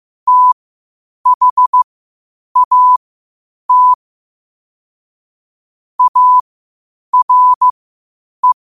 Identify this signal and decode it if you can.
Morse code, heard as a keyed tone, about 15 wpm, spaced as for 8 wpm, 1000 Hz. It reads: THAT ARE
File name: qotdfiles / 8wpm